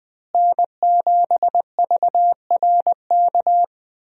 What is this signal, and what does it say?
Morse code, heard as a keyed tone, about 20 wpm, 700 Hz. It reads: N7VRK
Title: USA Call Signs - 5 Characters - Repeated 2x - 20wpm